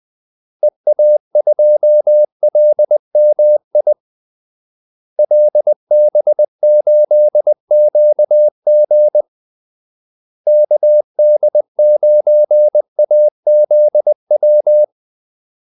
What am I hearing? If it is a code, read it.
Morse code, heard as a keyed tone, about 20 wpm, 600 Hz. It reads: EA2LMI LB8QG KD9AZW
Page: Kaldesignaler 20 wpm